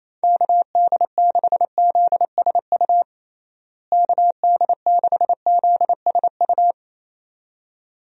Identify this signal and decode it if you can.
Morse code, heard as a keyed tone, about 27 wpm, 700 Hz. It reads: KD6ZSU KD6ZSU
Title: USA Call Signs - 6 Characters - Repeated 2x - 28wpm